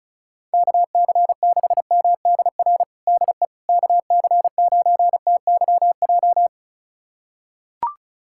Morse code, heard as a keyed tone, about 35 wpm, 700 Hz. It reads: KC6MDR DE KC9TYJ
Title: USA Callsign DE Callsign - 6 Characters - 35wpm